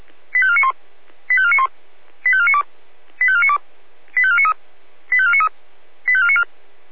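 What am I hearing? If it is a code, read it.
Five-tone selective call (ZVEI), heard as quick runs of five tones, tones of about 70 ms. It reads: 85491 85491 85492 85492 85493 85493 85494
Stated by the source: ZVEI five-tone call received by SDR.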